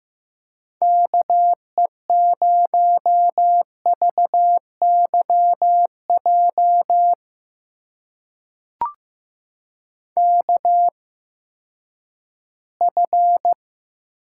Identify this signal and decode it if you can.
Morse code, heard as a keyed tone, about 15 wpm, spaced as for 10 wpm, 700 Hz. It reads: KE0VYJ KF
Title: USA Call Signs - 6 Characters - 8x Character Spacing - 15wpm